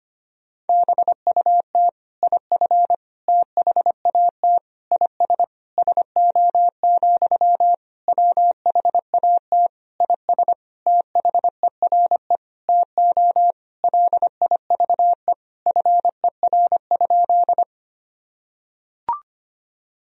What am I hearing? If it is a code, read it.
Morse code, heard as a keyed tone, about 25 wpm, 700 Hz. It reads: BUT IF THAT IS SO, WHAT IS THERE TO LIVE FER?
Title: Sentences from Top 200 Words and Abbreviations 25wpm